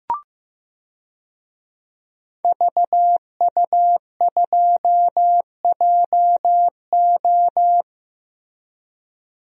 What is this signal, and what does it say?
Morse code, heard as a keyed tone, about 16 wpm, 700 Hz. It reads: VU2JO
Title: DXCC-100 Call Signs - 5 Characters - Repeated 3x - 15wpm